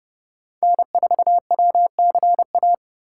Morse code, heard as a keyed tone, about 30 wpm, 700 Hz. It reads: N4WCA
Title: USA Call Signs - 5 Characters - 6x Character Spacing - 30wpm